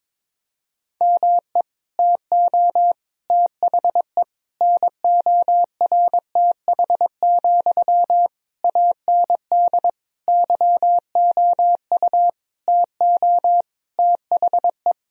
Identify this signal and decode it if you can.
Morse code, heard as a keyed tone, about 22 wpm, 700 Hz. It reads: ME TO THE NORTH, AND YOU TO THE
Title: Sentences from Top 1000 Words - Review Part 2 - 22wpm